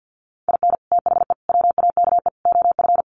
Morse code, heard as a keyed tone, about 50 wpm, 700 Hz. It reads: IN THE WAKE OF
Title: Sets of 4 Words 50wpm